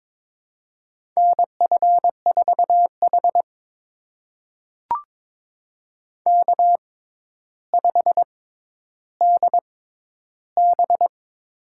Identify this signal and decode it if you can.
Morse code, heard as a keyed tone, about 22 wpm, spaced as for 11 wpm, 700 Hz. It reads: NF4H K5DB
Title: USA Call Signs - 4 Characters - 6x Character Spacing - 22wpm